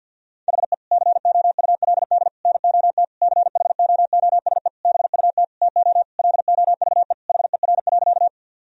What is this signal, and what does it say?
Morse code, heard as a keyed tone, about 50 wpm, 700 Hz. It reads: HE COULD NOT CHOOSE BUT TO LOVE HER.
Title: Sentences from Top 1000 Words - Review Part 4 - 50wpm